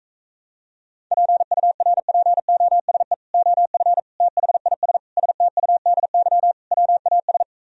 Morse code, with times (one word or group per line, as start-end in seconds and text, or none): 1.1-3.1 PURPOSE
3.3-4.0 OF
4.2-5.0 THIS
5.2-6.5 STUDY
6.7-7.4 WAS